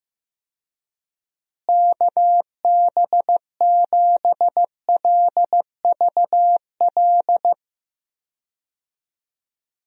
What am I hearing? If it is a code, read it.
Morse code, heard as a keyed tone, about 15 wpm, 700 Hz. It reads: KB7LVL
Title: USA Call Signs - 6 Characters - Repeated 3x - 15wpm